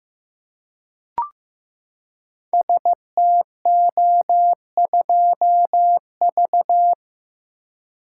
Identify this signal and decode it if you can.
Morse code, heard as a keyed tone, about 15 wpm, 700 Hz. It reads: STO2V
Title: Code Groups - 5 Letter:Numbers - 15wpm